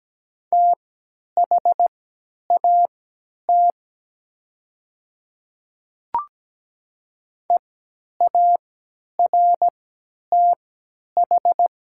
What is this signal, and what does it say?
Morse code, heard as a keyed tone, about 17 wpm, spaced as for 10 wpm, 700 Hz. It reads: THAT EARTH
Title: 3x Character Spacing - Top 500 Words - 17wpm